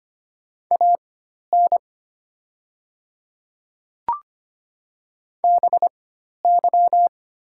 Morse code, heard as a keyed tone, about 25 wpm, spaced as for 12 wpm, 700 Hz. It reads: AN BY
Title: 4x Character Spacing - Top 300 Words - 25wpm